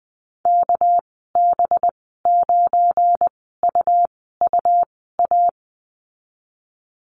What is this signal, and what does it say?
Morse code, heard as a keyed tone, about 20 wpm, spaced as for 14 wpm, 700 Hz. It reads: KB9UUA
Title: 231 Lesson 37g - Letter x - Call Signs 20wpm